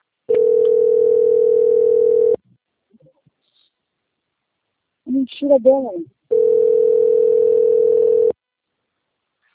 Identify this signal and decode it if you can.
Ringback tone (North America), 440 + 480 Hz, with speech.
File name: out- - - - - .42912